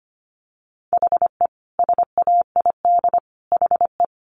Morse code, heard as a keyed tone, about 25 wpm, 700 Hz. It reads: HE SAID HE
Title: Sentences from Top 1500 Words 25wpm